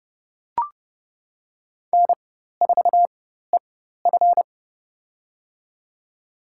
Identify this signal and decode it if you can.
Morse code, heard as a keyed tone, about 32 wpm, spaced as for 14 wpm, 700 Hz. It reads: N4EF
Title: USA Call Signs - 4x Character Spacing - 30wpm